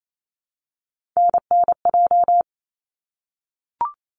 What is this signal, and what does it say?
Morse code, heard as a keyed tone, about 28 wpm, 700 Hz. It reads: NNJ
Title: ARRL Sections 28wpm